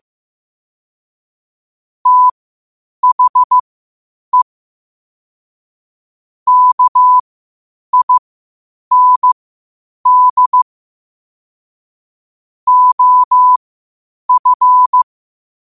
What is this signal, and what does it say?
Morse code, heard as a keyed tone, about 15 wpm, spaced as for 8 wpm, 1000 Hz. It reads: THE KIND OF
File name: qotdfiles / 8wpm